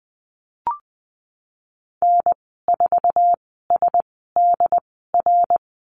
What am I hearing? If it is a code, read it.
Morse code, heard as a keyed tone, about 20 wpm, spaced as for 14 wpm, 700 Hz. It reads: N4SDR
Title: 069 Lesson 12g - Letter d - Call Signs 20wpm